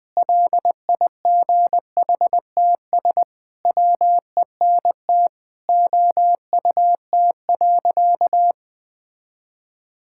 Morse code, heard as a keyed tone, about 20 wpm, 700 Hz. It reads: LIGHTS WENT OUT.
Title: Sentences with 6-Word Phrases - 20wpm